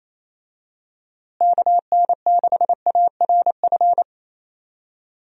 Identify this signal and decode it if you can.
Morse code, heard as a keyed tone, about 27 wpm, 700 Hz. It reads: KN6ARF
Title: USA Call Signs - 6 Characters - Repeated 3x - 28wpm